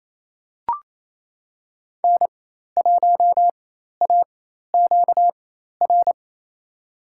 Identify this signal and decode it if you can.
Morse code, heard as a keyed tone, about 28 wpm, spaced as for 13 wpm, 700 Hz. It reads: N1AQR
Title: USA Call Signs - 5 Characters - 4x Character Spacing - 28wpm